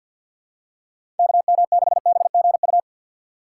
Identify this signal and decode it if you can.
Morse code, heard as a keyed tone, about 47 wpm, 700 Hz. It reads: KM6BGU
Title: USA Call Signs - 6 Characters - 6x Character Spacing - 50wpm